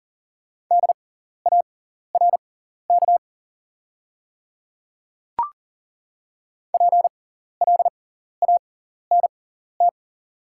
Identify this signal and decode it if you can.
Morse code, heard as a keyed tone, about 40 wpm, spaced as for 14 wpm, 700 Hz. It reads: DARK PLANT
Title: 6x Character Spacing - Top 500 Words - 40wpm